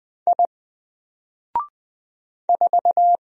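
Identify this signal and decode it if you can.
Morse code, heard as a keyed tone, about 20 wpm, 700 Hz. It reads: I 4
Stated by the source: Single letter and number Morse drill at 20 wpm, from N4LCD.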